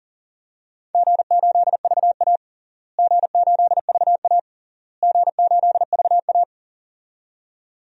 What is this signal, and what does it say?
Morse code, heard as a keyed tone, about 40 wpm, 700 Hz. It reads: G8VA G8VA G8VA
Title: DXCC-100 Call Signs - 4 Characters - Repeated 3x - 40wpm